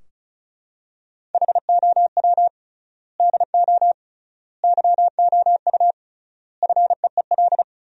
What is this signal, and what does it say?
Morse code, heard as a keyed tone, about 34 wpm, 700 Hz. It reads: HOW DO YOU FEEL
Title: 3x Word-Spacing - Sentences from Top 400 Words - 35wpm